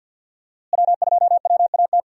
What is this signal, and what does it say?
Morse code, heard as a keyed tone, about 47 wpm, 700 Hz. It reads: W2WAT
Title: USA Call Signs - 5 Characters - Repeated 3x - 50wpm